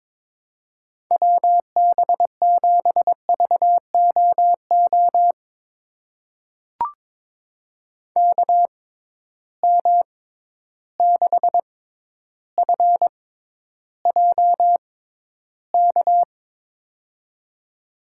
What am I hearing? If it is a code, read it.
Morse code, heard as a keyed tone, about 22 wpm, spaced as for 11 wpm, 700 Hz. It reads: WB7VOO KM6FJK
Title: USA Call Signs - 6x Character Spacing - 22wpm